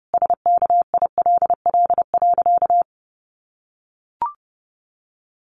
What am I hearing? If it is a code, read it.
Morse code, heard as a keyed tone, about 28 wpm, 700 Hz. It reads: SKILL.
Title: Sentences from Top 1200 Words 30wpm